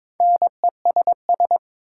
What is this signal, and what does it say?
Morse code, heard as a keyed tone, about 22 wpm, 700 Hz. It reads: NESS
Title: Prefixes and Suffixes 22wpm